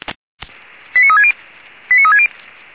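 Five-tone selective call (ZVEI), heard as quick runs of five tones, tones of about 70 ms. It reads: 89360 89360
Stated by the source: ZVEI five-tone call received by SDR, from the audio archive of the Orlamünde fire brigade (feuerwehr-orlamuende.org).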